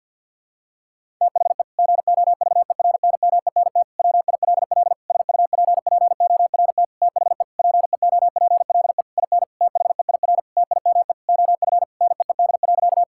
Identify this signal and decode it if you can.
Morse code, heard as a keyed tone, about 52 wpm, 700 Hz. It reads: THE GOVERNMENT WILL SUPPORT THE PEOPLE IN THEIR TIME OF NEED.
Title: Sentences from Top 1000 Words - Review Part 5 - 50wpm